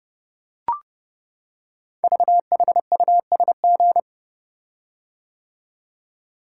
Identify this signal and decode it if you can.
Morse code, heard as a keyed tone, about 29 wpm, 700 Hz. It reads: VHUSG
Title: Code Groups - 5 Letters - 30wpm